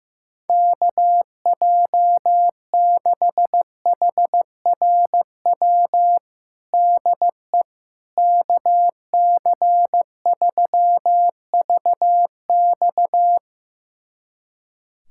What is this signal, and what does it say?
Morse code, heard as a keyed tone, about 16 wpm, 700 Hz. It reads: KJ6HRW DE KC3VX
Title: USA Callsign DE Callsign 15wpm